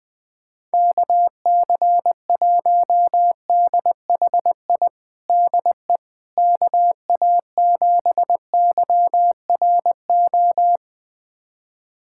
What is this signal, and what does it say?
Morse code, heard as a keyed tone, about 20 wpm, 700 Hz. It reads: KC1DHI DE KA7YRO
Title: USA Callsign DE Callsign - 6 Characters - 20wpm